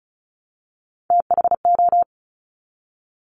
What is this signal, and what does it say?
Morse code, heard as a keyed tone, about 36 wpm, 700 Hz. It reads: THO
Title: Top Three Letter Combinations in Words 35wpm